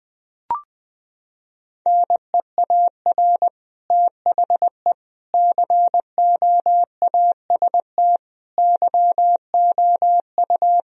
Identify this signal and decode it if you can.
Morse code, heard as a keyed tone, about 20 wpm, 700 Hz. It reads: NEAR THE COAST YOU
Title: Sentences from Top 1000 Words - Review Part 4 - 20wpm